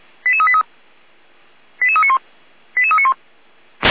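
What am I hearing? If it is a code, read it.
Five-tone selective call (ZVEI), heard as quick runs of five tones, tones of about 70 ms. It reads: 88383 88391 88391
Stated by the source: ZVEI five-tone call 88355 received via SDR.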